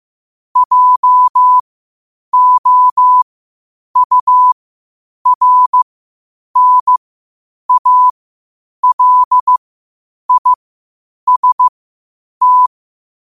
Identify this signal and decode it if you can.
Morse code, heard as a keyed tone, about 15 wpm, spaced as for 8 wpm, 1000 Hz. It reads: JOURNALIST
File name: 8wpm